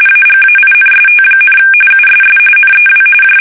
RTTY, about 45.45 baud. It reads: AGAINST, USS NEW J
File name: fsk850 / battleshipnj